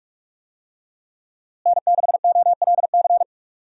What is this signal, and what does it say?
Morse code, heard as a keyed tone, about 41 wpm, 700 Hz. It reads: N6OLC
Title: USA Call Signs - 5 Characters - Repeated 2x - 45wpm